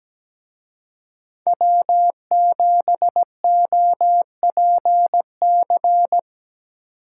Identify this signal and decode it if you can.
Morse code, heard as a keyed tone, about 20 wpm, 700 Hz. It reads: W7OPC
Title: Speed Racing - USA 1x3 Call Signs 17wpm